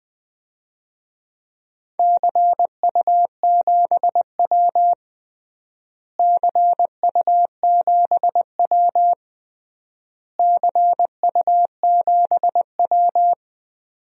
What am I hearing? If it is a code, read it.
Morse code, heard as a keyed tone, about 20 wpm, 700 Hz. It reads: CU7W CU7W CU7W
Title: DXCC-100 Call Signs - 4 Characters - Repeated 3x - 20wpm